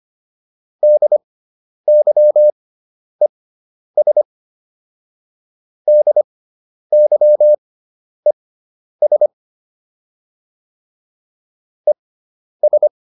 Morse code, heard as a keyed tone, about 25 wpm, spaced as for 10 wpm, 600 Hz. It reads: DYES DYES ES
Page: fundamental-practice-files